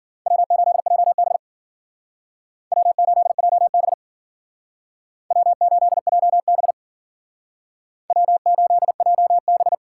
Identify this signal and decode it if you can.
Morse code, heard as a keyed tone, about 52 wpm, 700 Hz. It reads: W8JB W8JB W8JB W8JB